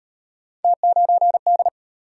Morse code, heard as a keyed tone, about 38 wpm, 700 Hz. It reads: T9D